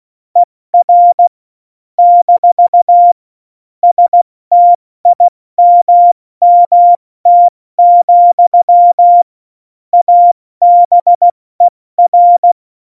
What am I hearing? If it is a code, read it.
Morse code, heard as a keyed tone, about 16 wpm, 700 Hz. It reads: ER - STIMMT, ABER
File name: Morsekurs / Witze / Witz 26. Januar 2021, 70 F80